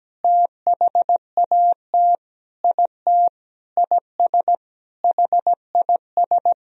Morse code, heard as a keyed tone, about 17 wpm, 700 Hz. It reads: THAT IT IS HIS
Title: Sentences from Top 1000 Words - Review Part 2 - 17wpm